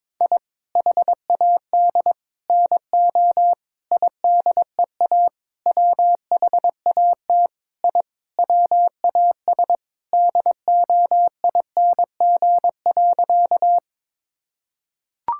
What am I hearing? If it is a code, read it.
Morse code, heard as a keyed tone, about 22 wpm, 700 Hz. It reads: I HAD NO IDEA WHAT I WAS DOING.
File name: practice / 22wpm / Sentences with 8-Word Phrases - 22wpm